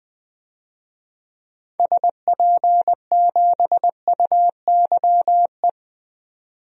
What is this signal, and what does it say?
Morse code, heard as a keyed tone, about 20 wpm, 700 Hz. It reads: SP7UYE
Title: DXCC-100 Call Signs - 6 Characters - 20wpm